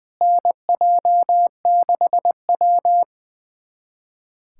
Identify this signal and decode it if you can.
Morse code, heard as a keyed tone, about 20 wpm, 700 Hz. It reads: NJ6W
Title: USA Callsign DE Callsign 20wpm